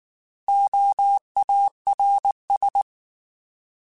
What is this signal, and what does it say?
Morse code, heard as a keyed tone, about 19 wpm, 800 Hz. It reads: OARS
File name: Alphabetical / Alphabetical mp3s / Os @18